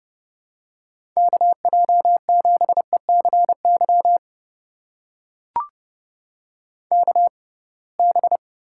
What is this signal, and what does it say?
Morse code, heard as a keyed tone, about 32 wpm, spaced as for 23 wpm, 700 Hz. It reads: KJ7ECY KB
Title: USA Call Signs - 6 Characters - 6x Character Spacing - 30wpm